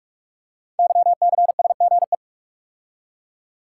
Morse code, heard as a keyed tone, about 44 wpm, 700 Hz. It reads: YCSGE
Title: Code Groups - 5 Letters - 45wpm